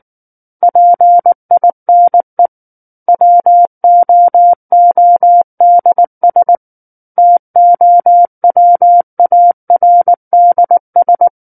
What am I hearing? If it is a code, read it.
Morse code, heard as a keyed tone, about 19 wpm, 700 Hz. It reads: PINE WOODS TOWARDS
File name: Book One / 10-Chapter 10 - 19 WPM